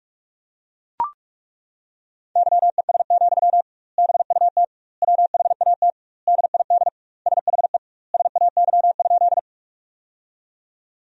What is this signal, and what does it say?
Morse code, heard as a keyed tone, about 41 wpm, 700 Hz. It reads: YES, BUT WHAT DID SHE SAY?
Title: 2x Word-Spacing - Sentences from Top 500 Words - 45wpm